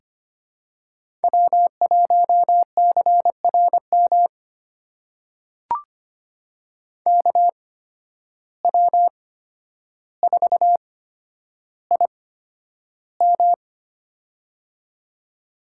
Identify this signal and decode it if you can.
Morse code, heard as a keyed tone, about 25 wpm, spaced as for 11 wpm, 700 Hz. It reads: W1CRM KW4IM
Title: USA Call Signs - 5 Characters - 8x Character Spacing - 25wpm